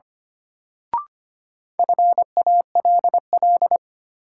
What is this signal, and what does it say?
Morse code, heard as a keyed tone, about 25 wpm, 700 Hz. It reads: FALL
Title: Sentences from Top 500 Words - New Word Drill - 25wpm